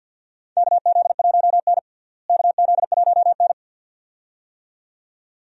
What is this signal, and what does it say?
Morse code, heard as a keyed tone, about 54 wpm, 700 Hz. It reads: KZ1N KZ1N
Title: USA Call Signs - 4 Characters - Repeated 2x - 50wpm